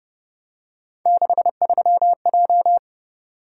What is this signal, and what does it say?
Morse code, heard as a keyed tone, about 28 wpm, 700 Hz. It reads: 63J